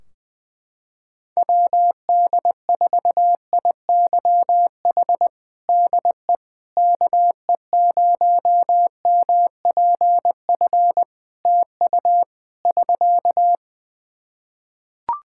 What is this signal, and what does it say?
Morse code, heard as a keyed tone, about 21 wpm, 700 Hz. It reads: WD4IYH DE KE0MPF TU <SK>
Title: Practice QSO - Exchange 4 - Ending - 20wpm